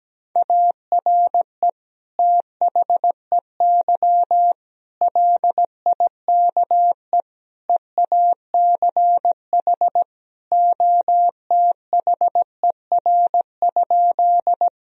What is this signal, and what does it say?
Morse code, heard as a keyed tone, about 17 wpm, 700 Hz. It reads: ARE THEY LIKE EACH OTHER?
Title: Sentences from Top 1000 Words - Review Part 1 - 17wpm